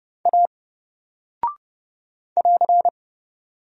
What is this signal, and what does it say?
Morse code, heard as a keyed tone, about 30 wpm, 700 Hz. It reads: A <AR>
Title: Single Character (Letter:Number:Punctuation:Prosign) 30wpm